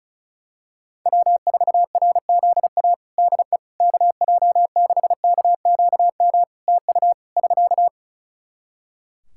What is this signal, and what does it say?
Morse code, heard as a keyed tone, about 36 wpm, 700 Hz. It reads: W4RZA DE KJ6KQM TU <SK>
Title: Practice QSO - Exchange 4 - Ending - 35wpm